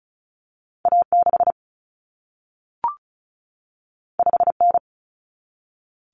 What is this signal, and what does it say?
Morse code, heard as a keyed tone, about 36 wpm, 700 Hz. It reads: A6 5N
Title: Common Cut Numbers - 2-Digit 35wpm